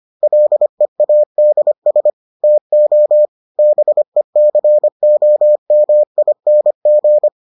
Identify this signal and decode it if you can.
Morse code, heard as a keyed tone, about 25 wpm, 600 Hz. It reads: LEADS TO BECOMING